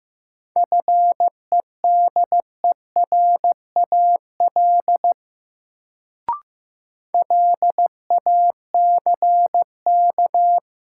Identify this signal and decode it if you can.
Morse code, heard as a keyed tone, about 15 wpm, 700 Hz. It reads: FEDERAL LACK
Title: Sets of 3 Words - New Word Drill - 15wpm